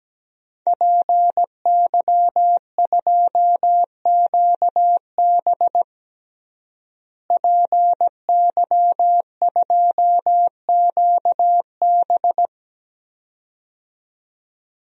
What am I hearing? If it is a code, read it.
Morse code, heard as a keyed tone, about 17 wpm, 700 Hz. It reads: PY2QB PY2QB
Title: DXCC-100 Call Signs - 5 Characters - Repeated 2x - 17wpm